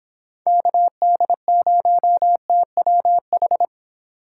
Morse code, heard as a keyed tone, about 22 wpm, 700 Hz. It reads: KD0TWH